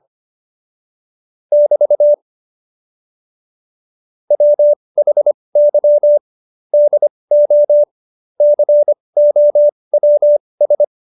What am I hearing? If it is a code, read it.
Morse code, heard as a keyed tone, about 25 wpm, spaced as for 20 wpm, 590 Hz. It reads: <BT> WHY DO COWS
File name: Jokes-25-char-20-eff